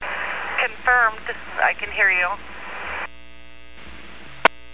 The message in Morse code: none

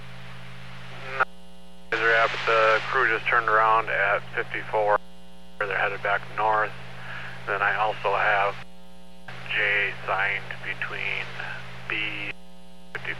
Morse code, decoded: none

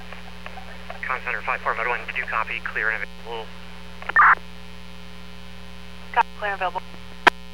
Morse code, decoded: none

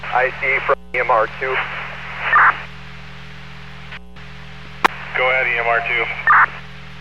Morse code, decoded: none

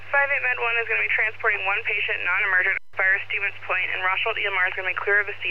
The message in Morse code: none

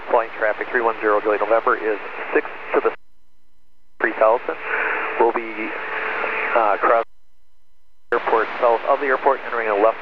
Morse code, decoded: none

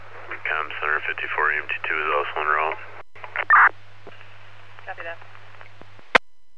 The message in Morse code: none

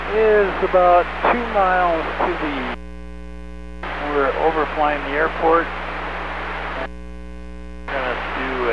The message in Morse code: none

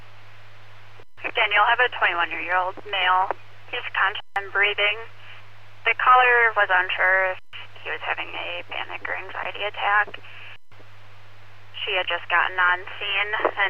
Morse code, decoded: none

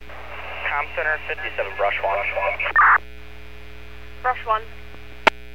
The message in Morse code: none

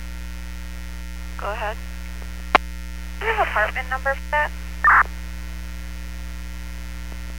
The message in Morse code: none

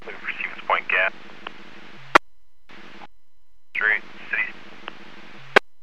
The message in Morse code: none

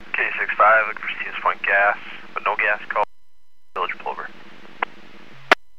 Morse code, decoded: none

